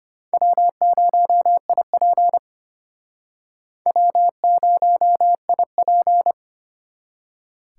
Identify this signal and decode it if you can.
Morse code, heard as a keyed tone, about 34 wpm, 700 Hz. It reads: W0IP W0IP